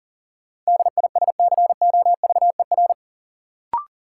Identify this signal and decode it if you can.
Morse code, heard as a keyed tone, about 40 wpm, 700 Hz. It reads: DISCOVER
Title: Sentences from Top 1100 Words - New Word Drill Alt - 40wpm